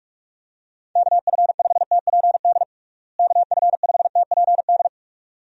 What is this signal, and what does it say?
Morse code, heard as a keyed tone, about 46 wpm, 700 Hz. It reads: KF5TPD KF5TPD
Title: USA Call Signs - 6 Characters - Repeated 2x - 45wpm